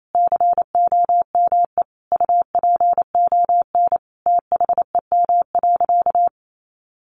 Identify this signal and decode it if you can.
Morse code, heard as a keyed tone, about 28 wpm, 700 Hz. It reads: COME UPON THEM.
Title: Sentences from Top 1200 Words 28wpm